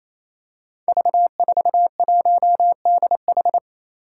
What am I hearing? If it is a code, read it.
Morse code, heard as a keyed tone, about 28 wpm, 700 Hz. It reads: V41DH
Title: Five Challenging Letter-Number Combinations 28wpm